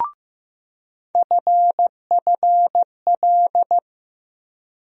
Morse code, heard as a keyed tone, about 15 wpm, 700 Hz. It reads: FFL
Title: USA Call Sign Postfixes - 3 Letters 15wpm